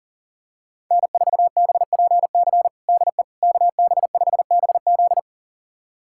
Morse code, heard as a keyed tone, about 40 wpm, 700 Hz. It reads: N4BPC DE KB5BZ
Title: USA Callsign DE Callsign - 5 Characters - 40wpm